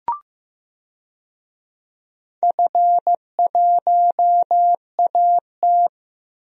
Morse code, heard as a keyed tone, about 15 wpm, 700 Hz. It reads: F1AT
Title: DXCC-100 Call Signs - 4 Characters - Repeated 3x - 15wpm